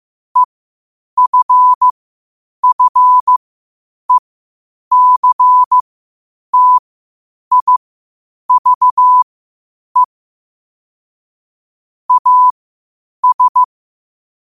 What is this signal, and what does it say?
Morse code, heard as a keyed tone, about 15 wpm, spaced as for 8 wpm, 1000 Hz. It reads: EFFECTIVE AS
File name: qotdfiles / 8wpm